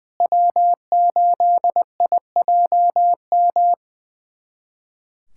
Morse code, heard as a keyed tone, about 20 wpm, 700 Hz. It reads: W8IJM
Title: USA Callsign DE Callsign - 5 Characters - 20wpm